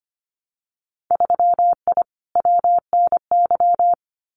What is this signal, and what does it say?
Morse code, heard as a keyed tone, about 25 wpm, 700 Hz. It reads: 3I WNY
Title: Winter Field Day Exchanges 25wpm